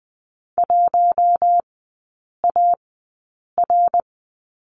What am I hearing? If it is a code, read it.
Morse code, heard as a keyed tone, about 20 wpm, spaced as for 8 wpm, 700 Hz. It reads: 1AR
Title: 053 Lesson 10e - Letter r - Three Characters 20wpm